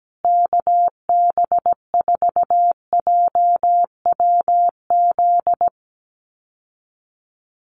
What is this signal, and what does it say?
Morse code, heard as a keyed tone, about 17 wpm, 700 Hz. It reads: KB4JWZ
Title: USA Call Signs Repeated 2x 17wpm